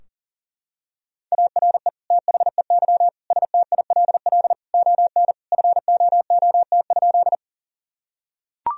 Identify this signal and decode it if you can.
Morse code, heard as a keyed tone, about 40 wpm, 700 Hz. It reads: ARE THEY STILL ON FOOT?
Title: Sentences from Top 1000 Words - Review Part 3 - 40wpm